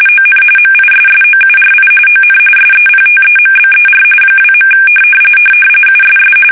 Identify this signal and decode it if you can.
RTTY, about 45.45 baud. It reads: ON MISSOURI ON SEPTEMBER 2. USS
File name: fsk850 / battleshipnj